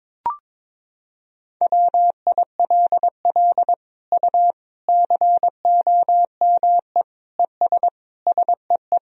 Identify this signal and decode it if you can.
Morse code, heard as a keyed tone, about 22 wpm, 700 Hz. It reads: WILL U COME ES SEE
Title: Sentences from Top 100 Words and Abbreviations 22wpm